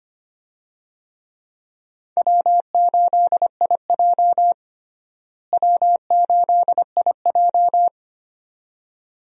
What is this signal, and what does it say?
Morse code, heard as a keyed tone, about 25 wpm, 700 Hz. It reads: W8IJ W8IJ
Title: USA Call Signs - 4 Characters - Repeated 2x - 25wpm